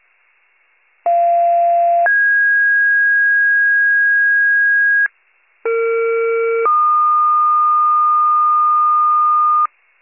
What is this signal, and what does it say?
Two-tone page, heard as a one-second tone, then a three-second tone, about 690 then 1650 Hz, about 460 then 1220 Hz, with noise. It is heard from 1.1 to 9.7 s.